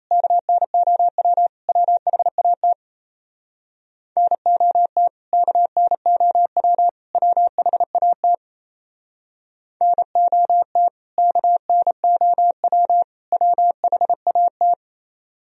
Morse code, heard as a keyed tone, about 34 wpm, 700 Hz. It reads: KNOW WHAT NOT KNOW WHAT NOT KNOW WHAT
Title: Speed-Racing - Sets of 3 Words 28wpm